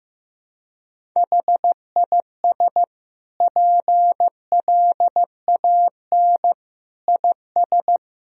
Morse code, heard as a keyed tone, about 14 wpm, 700 Hz. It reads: HIS PLAN IS
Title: Sentences from Top 500 Words and Abbreviations 15wpm